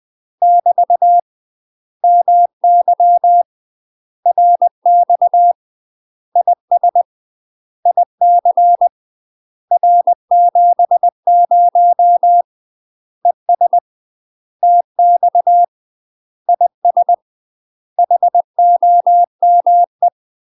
Morse code, heard as a keyed tone, about 20 wpm, 700 Hz. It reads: <BT> MY RX IS IC R70 ES TX IS HOME
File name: UK MASTER QSOs TRAINING / 20WPM / UK MASTER QSOs 0007